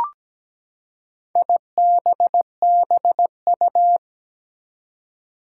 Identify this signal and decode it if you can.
Morse code, heard as a keyed tone, about 17 wpm, 700 Hz. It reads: IBBU
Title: Four Challenging Letter-Number Combinations 17wpm